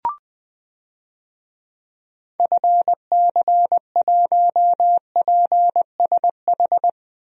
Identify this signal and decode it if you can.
Morse code, heard as a keyed tone, about 20 wpm, 700 Hz. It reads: FC1PSH
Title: DXCC-100 Call Signs - 6 Characters - Repeated 2x - 20wpm